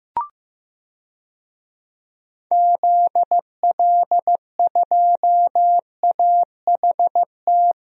Morse code, heard as a keyed tone, about 15 wpm, 700 Hz. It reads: ZL2AHT
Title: DXCC-100 Call Signs - 6 Characters - 15wpm